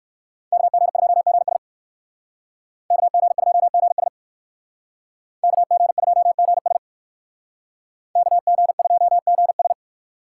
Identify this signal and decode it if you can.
Morse code, heard as a keyed tone, about 65 wpm, 700 Hz. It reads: KG2GS KG2GS KG2GS KG2GS